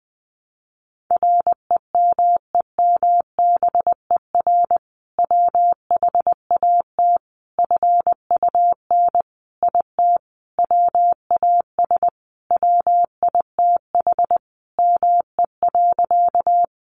Morse code, heard as a keyed tone, about 20 wpm, 700 Hz. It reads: REMEMBER WHAT FUN IT WAS WITH ME.
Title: Sentences from Top 700 Words 20wpm